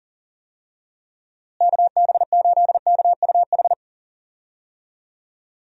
Morse code, heard as a keyed tone, about 37 wpm, 700 Hz. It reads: KB8KUH
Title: USA Call Signs - 6 Characters - Repeated 2x - 40wpm